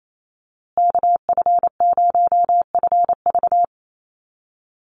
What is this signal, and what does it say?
Morse code, heard as a keyed tone, about 28 wpm, 700 Hz. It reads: KF0FV
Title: USA Call Signs Repeated 3x 28wpm